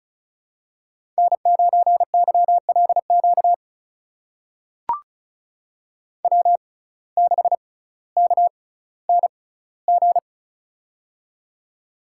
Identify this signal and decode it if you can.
Morse code, heard as a keyed tone, about 36 wpm, spaced as for 18 wpm, 700 Hz. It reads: N9YLQ W6KNG
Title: USA Call Signs - 6x Character Spacing - 35wpm